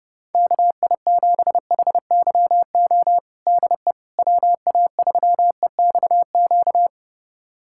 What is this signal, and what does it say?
Morse code, heard as a keyed tone, about 30 wpm, 700 Hz. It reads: KI7HYO DE WA3EXQ
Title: USA Callsign DE Callsign - 6 Characters - 30wpm